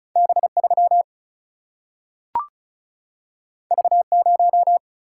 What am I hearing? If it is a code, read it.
Morse code, heard as a keyed tone, about 36 wpm, 700 Hz. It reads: B 3 V 0
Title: Single-Letter Single-Number 35wpm